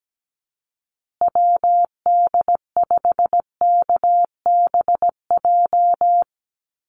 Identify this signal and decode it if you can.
Morse code, heard as a keyed tone, about 17 wpm, 700 Hz. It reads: WD5KBJ
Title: USA Call Signs Repeated 2x 17wpm